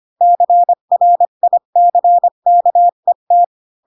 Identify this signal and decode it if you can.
Morse code, heard as a keyed tone, about 25 wpm, 700 Hz. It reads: CRICKET
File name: Pinocchio / Pinocchio ch7 20 25 ik1hge